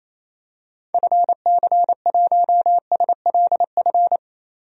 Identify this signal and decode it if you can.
Morse code, heard as a keyed tone, about 28 wpm, 700 Hz. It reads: FC1SLF
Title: DXCC-100 Call Signs - 6 Characters - 28wpm